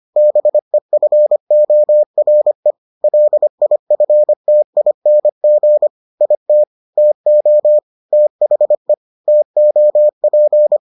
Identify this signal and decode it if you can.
Morse code, heard as a keyed tone, about 25 wpm, 600 Hz. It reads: BEFORE LIFTING IT TO THE TOP